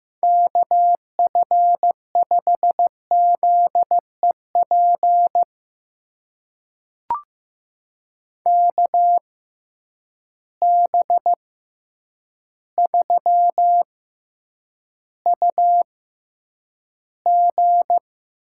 Morse code, heard as a keyed tone, about 15 wpm, spaced as for 8 wpm, 700 Hz. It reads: KF5ZEP KB3UG
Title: USA Call Signs - 6 Characters - 6x Character Spacing - 15wpm